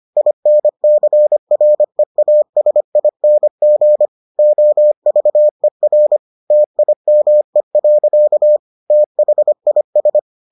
Morse code, heard as a keyed tone, about 25 wpm, 600 Hz. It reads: INCREASING OVER TIME. THIS